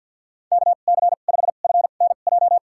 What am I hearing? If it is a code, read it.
Morse code, heard as a keyed tone, about 51 wpm, spaced as for 36 wpm, 700 Hz. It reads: KC5FNJ
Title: USA Call Signs - 2x Character Spacing - 50wpm